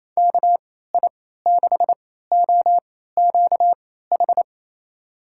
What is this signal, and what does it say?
Morse code, heard as a keyed tone, about 28 wpm, spaced as for 16 wpm, 700 Hz. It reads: KI6OQH
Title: USA Call Signs - 6 Characters - 3x Character Spacing - 28wpm